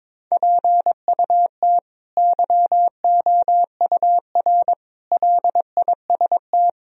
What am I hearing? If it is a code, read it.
Morse code, heard as a keyed tone, about 22 wpm, 700 Hz. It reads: PUT YOUR LIST
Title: Sentences from Top 1000 Words - Review Part 2 - 22wpm